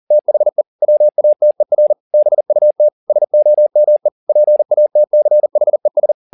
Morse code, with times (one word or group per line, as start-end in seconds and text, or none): 0.1-0.6 THE
0.8-1.9 WATER
2.1-2.9 BUT
3.1-4.1 SOME
4.3-6.1 PATCHES